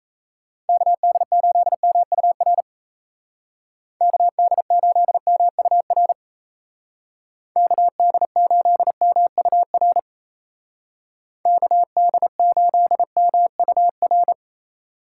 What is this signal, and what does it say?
Morse code, heard as a keyed tone, about 33 wpm, 700 Hz. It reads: KD8MUR KD8MUR KD8MUR KD8MUR
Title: Speed Racing - USA Call Signs 28wpm